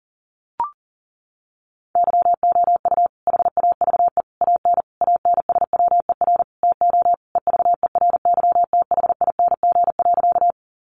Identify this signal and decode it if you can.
Morse code, heard as a keyed tone, about 40 wpm, 700 Hz. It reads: YOU HAVE AN ANSWER TO EVERYTHING.
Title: Sentences from Top 700 Words 40wpm